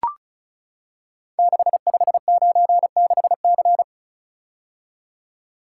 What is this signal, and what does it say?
Morse code, heard as a keyed tone, about 36 wpm, 690 Hz. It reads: 6596C
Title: Code Groups - 5 Letter:Numbers - 35wpm